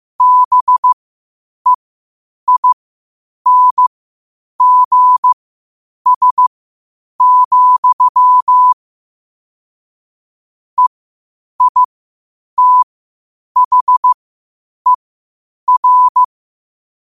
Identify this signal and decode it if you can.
Morse code, heard as a keyed tone, about 15 wpm, spaced as for 8 wpm, 1000 Hz. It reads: BEINGS, EITHER
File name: qotdfiles / 8wpm